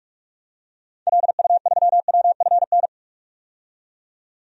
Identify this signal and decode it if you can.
Morse code, heard as a keyed tone, about 46 wpm, 700 Hz. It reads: LU3WFN
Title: DXCC-100 Call Signs - 6 Characters - 45wpm